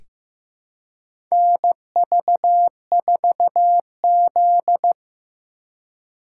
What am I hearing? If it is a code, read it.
Morse code, heard as a keyed tone, about 15 wpm, 700 Hz. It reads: NV4Z
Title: USA Call Signs - 3x Character Spacing - 15wpm